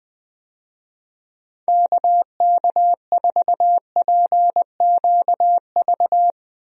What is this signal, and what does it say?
Morse code, heard as a keyed tone, about 20 wpm, 700 Hz. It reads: KK4PQV
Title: USA Call Signs - 6 Characters - Repeated 2x - 20wpm